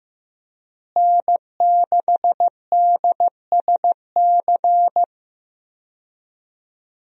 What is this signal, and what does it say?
Morse code, heard as a keyed tone, about 15 wpm, 700 Hz. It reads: N6DSC
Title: USA Call Signs - 5 Characters - Repeated 2x - 15wpm